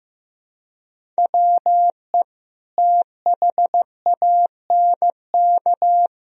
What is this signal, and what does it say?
Morse code, heard as a keyed tone, about 16 wpm, 700 Hz. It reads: WE THANK
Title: Sentences from Top 1000 Words - Review Part 1 - 15wpm